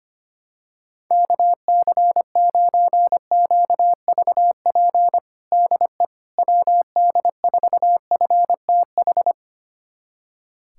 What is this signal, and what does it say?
Morse code, heard as a keyed tone, about 25 wpm, 700 Hz. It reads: KC9QVP DE WD4FTH
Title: USA Callsign DE Callsign - 6 Characters - 25wpm